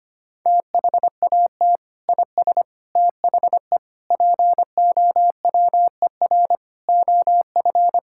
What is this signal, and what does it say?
Morse code, heard as a keyed tone, about 25 wpm, 700 Hz. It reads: THAT IS THE POWER OF
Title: Sentences from Top 1000 Words - Review Part 5 - 25wpm